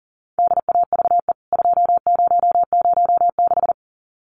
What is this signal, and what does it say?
Morse code, heard as a keyed tone, about 40 wpm, 700 Hz. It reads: DAVE 2006
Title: CWT Exchange 40wpm